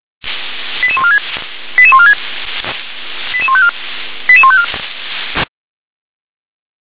Five-tone selective call (ZVEI), heard as quick runs of five tones, tones of about 70 ms. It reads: 88146 88146 88154 88154
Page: /357/SDR/ZVEI/88640